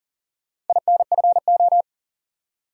Morse code, heard as a keyed tone, about 39 wpm, 700 Hz. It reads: INFO
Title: Top 180 QSO Elements 40wpm